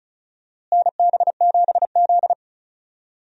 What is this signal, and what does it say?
Morse code, heard as a keyed tone, about 34 wpm, 700 Hz. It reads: NB7Z
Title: USA Call Signs - 4 Characters - Repeated 2x - 35wpm